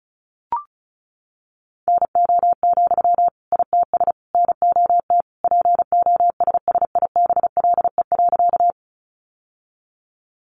Morse code, heard as a keyed tone, about 34 wpm, 700 Hz. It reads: NO, ITS NOT POSSIBLE.
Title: Sentences from Top 600 Words 35wpm